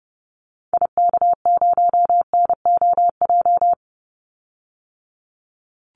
Morse code, heard as a keyed tone, about 30 wpm, 700 Hz. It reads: IK0NOJ
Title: Call Signs Repeated 2x - 30wpm